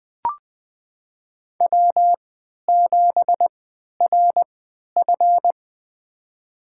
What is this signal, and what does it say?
Morse code, heard as a keyed tone, about 20 wpm, spaced as for 11 wpm, 700 Hz. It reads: W7RF
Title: USA Call Signs - 4 Characters - 3x Character Spacing - 20wpm